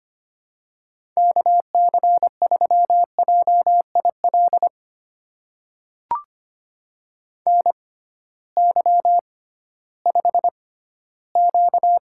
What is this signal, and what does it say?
Morse code, heard as a keyed tone, about 25 wpm, spaced as for 15 wpm, 700 Hz. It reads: KC3JIL NY5Q
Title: USA Call Signs - 6x Character Spacing - 25wpm